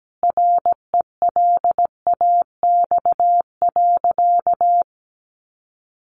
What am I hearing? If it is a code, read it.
Morse code, heard as a keyed tone, about 17 wpm, 700 Hz. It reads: RELAX.
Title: Sentences from Top 1400 Words 17wpm